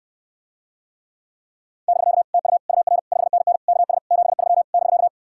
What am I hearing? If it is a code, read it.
Morse code, heard as a keyed tone, about 68 wpm, 700 Hz. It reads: <BK> TU UR 5NN KS 73 <BK>